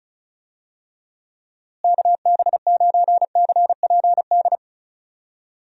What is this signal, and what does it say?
Morse code, heard as a keyed tone, about 36 wpm, 700 Hz. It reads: KB9CPD
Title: USA Call Signs - 6 Characters - Repeated 3x - 35wpm